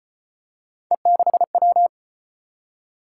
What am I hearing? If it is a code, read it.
Morse code, heard as a keyed tone, about 33 wpm, 700 Hz. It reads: E6W